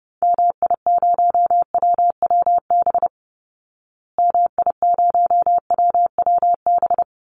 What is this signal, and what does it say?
Morse code, heard as a keyed tone, about 30 wpm, 700 Hz. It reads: MI0WWB MI0WWB
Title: Call Signs Repeated 2x - 30wpm